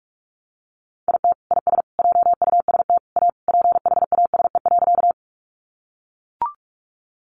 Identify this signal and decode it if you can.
Morse code, heard as a keyed tone, about 44 wpm, 700 Hz. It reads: IT IS JUST A PHASE.
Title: Sentences from Top 1200 Words 45wpm